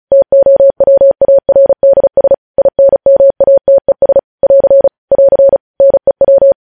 Morse code, heard as a keyed tone, about 36 wpm, 550 Hz. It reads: TOWARDS INMATES <AR> <AR> NEW
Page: News Headlines in Morse Code at 35 WPM